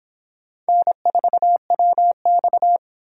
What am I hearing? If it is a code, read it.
Morse code, heard as a keyed tone, about 26 wpm, 700 Hz. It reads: N4WX